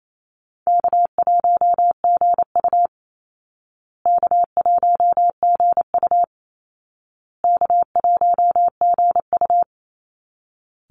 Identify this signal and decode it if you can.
Morse code, heard as a keyed tone, about 28 wpm, 700 Hz. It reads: K1GU K1GU K1GU
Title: Call Signs Repeated 2x - 28wpm